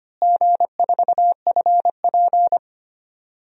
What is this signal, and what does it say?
Morse code, heard as a keyed tone, about 25 wpm, 700 Hz. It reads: G4FP
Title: DXCC-100 Call Signs - 4 Characters - Repeated 3x - 25wpm